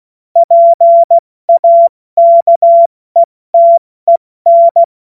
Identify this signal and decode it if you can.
Morse code, heard as a keyed tone, about 16 wpm, spaced as for 14 wpm, 680 Hz. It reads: PAKETEN
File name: Morsekurs / Witze / Witz 29. Juni 2021, 70 F80